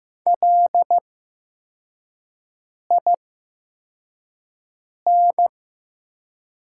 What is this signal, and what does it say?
Morse code, heard as a keyed tone, about 15 wpm, spaced as for 4 wpm, 700 Hz. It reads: LIN
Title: 8x Character Spacing - Top 200 Words - 15wpm